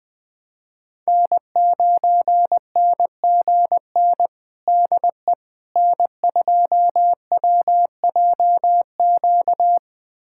Morse code, heard as a keyed tone, about 21 wpm, 700 Hz. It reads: N9NGN DE N2WJQ
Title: USA Callsign DE Callsign 20wpm